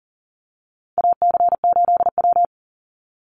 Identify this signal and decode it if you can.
Morse code, heard as a keyed tone, about 38 wpm, 700 Hz. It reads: AC8W
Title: Call Signs - 40wpm